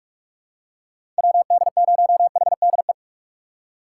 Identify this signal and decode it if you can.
Morse code, heard as a keyed tone, about 42 wpm, 700 Hz. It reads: WD0HDE
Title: USA Call Signs - 3x Character Spacing - 45wpm